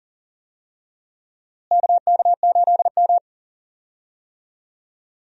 Morse code, heard as a keyed tone, about 40 wpm, 700 Hz. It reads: KK8M
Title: USA Call Signs - 4 Characters - Repeated 3x - 40wpm